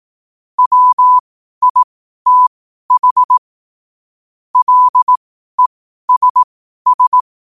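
Morse code, heard as a keyed tone, about 18 wpm, spaced as for 12 wpm, 1000 Hz. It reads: WITH LESS
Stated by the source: Morse code quote-of-the-day practice at 12 words per minute.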